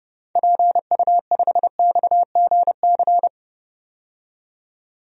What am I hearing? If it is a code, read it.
Morse code, heard as a keyed tone, about 29 wpm, 690 Hz. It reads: PU5XGC
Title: DXCC-100 Call Signs - 6 Characters - 30wpm